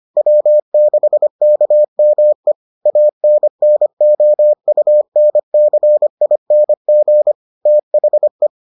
Morse code, heard as a keyed tone, about 25 wpm, 600 Hz. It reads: W6KME ANNOUNCING THE